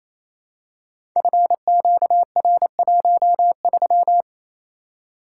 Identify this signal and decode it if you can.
Morse code, heard as a keyed tone, about 28 wpm, 700 Hz. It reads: FQR13
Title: Code Groups - 5 Letter:Numbers - 28wpm